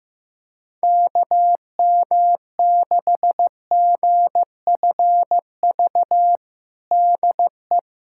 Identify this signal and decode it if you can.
Morse code, heard as a keyed tone, about 15 wpm, 700 Hz. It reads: KM6GFV DE
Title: USA Callsign DE Callsign 15wpm